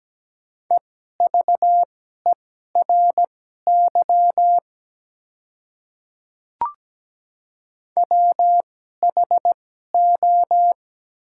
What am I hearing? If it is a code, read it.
Morse code, heard as a keyed tone, about 17 wpm, spaced as for 12 wpm, 700 Hz. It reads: EVERY WHO
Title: 2x Character Spacing - Top 200 Words - 17wpm